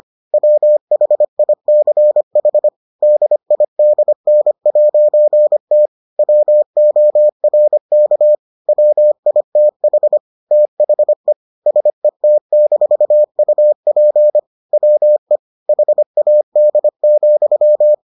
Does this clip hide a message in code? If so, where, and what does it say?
Yes: Morse code from 0.3 to 17.9 s, heard as a keyed tone, about 25 wpm, 600 Hz. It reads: WHICH DIDN'T WORK WITH THE SET-UP WE HAD,